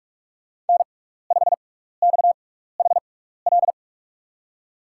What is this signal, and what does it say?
Morse code, heard as a keyed tone, about 47 wpm, spaced as for 16 wpm, 690 Hz. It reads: N5XHL
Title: USA Call Signs - 5 Characters - 6x Character Spacing - 45wpm